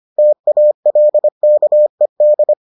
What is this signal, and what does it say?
Morse code, heard as a keyed tone, about 25 wpm, 600 Hz. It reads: TALKED